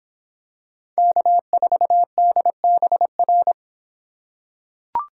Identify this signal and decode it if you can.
Morse code, heard as a keyed tone, about 24 wpm, 700 Hz. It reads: K4DBR